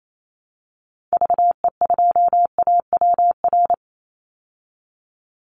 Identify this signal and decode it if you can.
Morse code, heard as a keyed tone, about 28 wpm, 700 Hz. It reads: VE2AWR
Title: Call Signs Repeated 3x - 28wpm